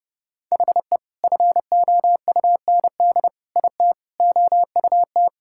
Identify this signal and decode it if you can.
Morse code, heard as a keyed tone, about 30 wpm, 700 Hz. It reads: HE FOUND IT OUT
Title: Sentences from Top 1600 Words 30wpm